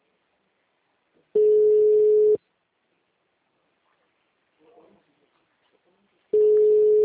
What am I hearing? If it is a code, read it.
Ringback tone (Europe, 425 Hz).
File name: out- -2008- - - .18116